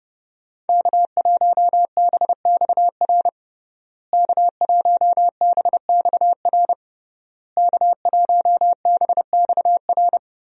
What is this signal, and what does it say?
Morse code, heard as a keyed tone, about 30 wpm, 700 Hz. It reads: K1BXR K1BXR K1BXR
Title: USA Call Signs - 5 Characters - Repeated 3x - 30wpm